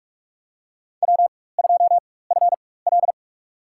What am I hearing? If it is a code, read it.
Morse code, heard as a keyed tone, about 45 wpm, spaced as for 21 wpm, 700 Hz. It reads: W2FL
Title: USA Call Signs - 4 Characters - 4x Character Spacing - 45wpm